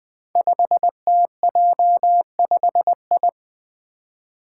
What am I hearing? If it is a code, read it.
Morse code, heard as a keyed tone, about 20 wpm, 700 Hz. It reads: 5TJ5I
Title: Five Challenging Letter-Number Combinations 20wpm